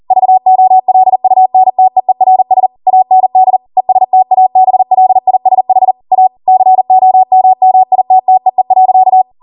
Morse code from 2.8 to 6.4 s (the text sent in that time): AND ESTABLISH A